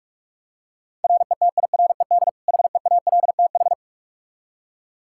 Morse code, heard as a keyed tone, about 46 wpm, 700 Hz. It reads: RETIRED HEALTH
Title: Practice QSO - Exchange 3 - More Info - 45wpm